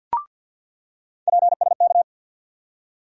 Morse code, heard as a keyed tone, about 46 wpm, 700 Hz. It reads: PSK
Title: USA Call Sign Postfixes - 3 Letters 50wpm